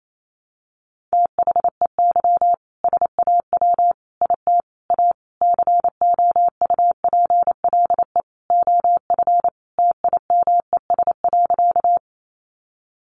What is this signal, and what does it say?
Morse code, heard as a keyed tone, about 28 wpm, 700 Hz. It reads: THEY SAW IT A COUPLE OF TIMES.
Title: Sentences from Top 1100 Words 28wpm